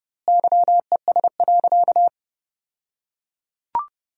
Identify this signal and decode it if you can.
Morse code, heard as a keyed tone, about 30 wpm, 700 Hz. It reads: YES.
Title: 2x Word-Spacing - Sentences from Top 500 Words - 30wpm